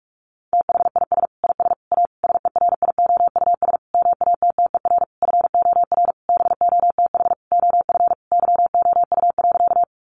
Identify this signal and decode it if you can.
Morse code, heard as a keyed tone, about 44 wpm, 700 Hz. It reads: THIS IS A SERIOUS MATTER FOR BOTH OF YOU.
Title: Sentences from Top 1100 Words 45wpm